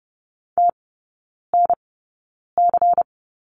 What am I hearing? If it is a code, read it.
Morse code, heard as a keyed tone, about 30 wpm, spaced as for 9 wpm, 690 Hz. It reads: TNC
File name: Lesson 16e - Letter c - Three Characters 30wpm